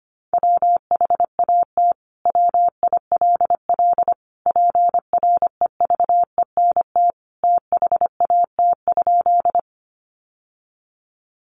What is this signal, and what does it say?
Morse code, heard as a keyed tone, about 25 wpm, 700 Hz. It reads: WHAT WILL PREVENT THAT?
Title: Sentences from Top 1100 Words 25wpm